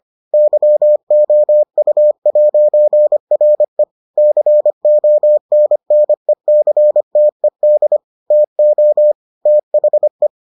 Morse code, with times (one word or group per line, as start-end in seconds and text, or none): 0.3-3.8 YOU'RE
4.2-8.0 CONNECTED
8.3-9.1 TO
9.5-10.3 THE